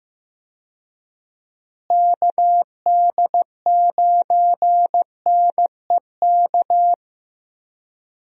Morse code, heard as a keyed tone, about 15 wpm, 700 Hz. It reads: KD9NEK
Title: USA Call Signs - 6 Characters - Repeated 3x - 15wpm